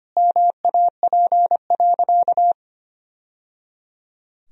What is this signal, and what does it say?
Morse code, heard as a keyed tone, about 25 wpm, 700 Hz. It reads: MAP.
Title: Sentences from Top 1000 Words - Review Part 2 - 25wpm